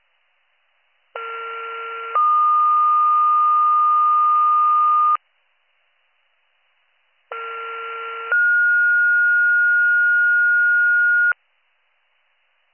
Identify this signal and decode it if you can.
Two-tone page, heard as a one-second tone, then a three-second tone, about 1370 then 1220 Hz, about 1370 then 1430 Hz. It is heard from 1.2 to 11.3 s.